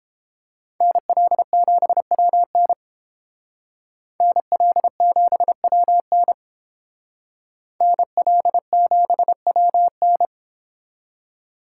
Morse code, heard as a keyed tone, about 30 wpm, 700 Hz. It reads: NL7WN NL7WN NL7WN